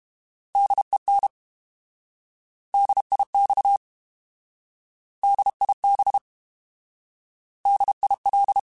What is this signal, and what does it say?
Morse code, heard as a keyed tone, about 32 wpm, 800 Hz. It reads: DEN DIX DIB DIL